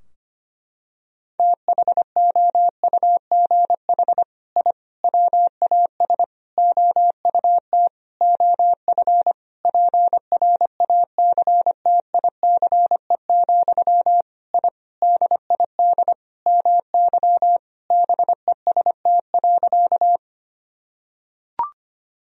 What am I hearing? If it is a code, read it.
Morse code, heard as a keyed tone, about 25 wpm, 700 Hz. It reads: THOUGH I WAS OUT OF PRACTICE, I DID MY BEST.
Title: Sentences from Top 1000 Words - Review Part 4 - 25wpm